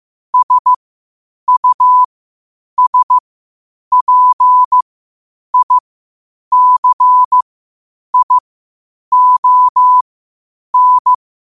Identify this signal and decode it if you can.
Morse code, heard as a keyed tone, about 15 wpm, spaced as for 8 wpm, 1000 Hz. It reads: SUSPICION
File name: qotdfiles / 8wpm